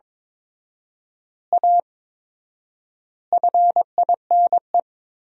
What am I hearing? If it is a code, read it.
Morse code, heard as a keyed tone, about 22 wpm, 700 Hz. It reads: A FINE
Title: 4x Word-Spacing - Sentences from Top 500 Words - 22wpm